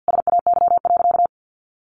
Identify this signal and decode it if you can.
Morse code, heard as a keyed tone, about 50 wpm, 700 Hz. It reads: SAY.
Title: Sentences from Top 500 Words 50wpm